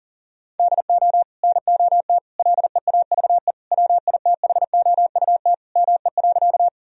Morse code, heard as a keyed tone, about 37 wpm, 700 Hz. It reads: DO NOT LEAVE WITHOUT ME.
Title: Sentences from Top 1000 Words - Review Part 3 - 40wpm